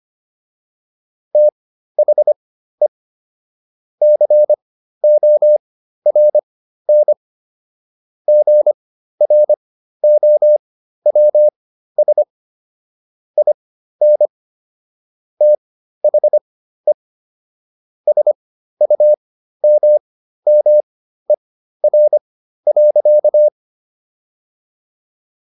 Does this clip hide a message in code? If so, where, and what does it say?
Yes: Morse code from 1.3 to 23.5 s, heard as a keyed tone, about 25 wpm, spaced as for 13 wpm, 600 Hz. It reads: THE CORN GROWS IN THE SUMMER.